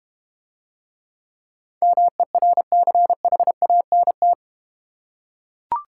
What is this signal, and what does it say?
Morse code, heard as a keyed tone, about 32 wpm, 700 Hz. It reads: MERCHANT